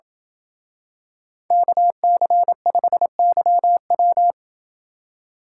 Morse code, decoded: KC5YW

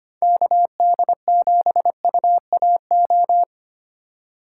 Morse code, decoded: KD7UAO

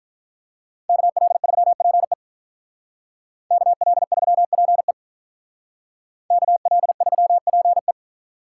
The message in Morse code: KL3PE KL3PE KL3PE

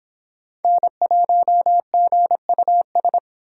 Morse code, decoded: N1GUS

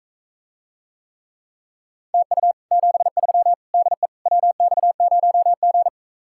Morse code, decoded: TU 73 DE WX0G

